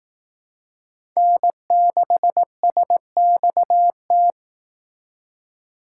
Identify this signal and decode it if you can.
Morse code, heard as a keyed tone, about 18 wpm, 700 Hz. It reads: N6SXT
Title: Speed Racing - USA Call Signs 15wpm